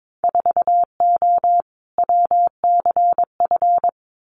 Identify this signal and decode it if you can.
Morse code, heard as a keyed tone, about 22 wpm, 700 Hz. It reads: 4O WCF
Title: Winter Field Day Exchanges Encore 22wpm